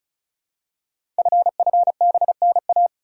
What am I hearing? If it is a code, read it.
Morse code, heard as a keyed tone, about 36 wpm, 700 Hz. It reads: FFBNA
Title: Code Groups - 5 Letters - 35wpm